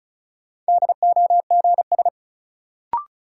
Morse code, heard as a keyed tone, about 36 wpm, 700 Hz. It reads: DOGS
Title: Sentences from Top 1600 Words - New Word Drill - 35wpm